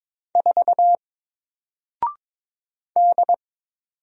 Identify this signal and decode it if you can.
Morse code, heard as a keyed tone, about 22 wpm, 700 Hz. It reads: 4 D
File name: Single-Letter-Number-22wpm